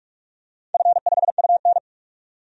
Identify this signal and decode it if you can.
Morse code, heard as a keyed tone, about 44 wpm, 700 Hz. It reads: F5UN
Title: DXCC-100 Call Signs - 4 Characters - 45wpm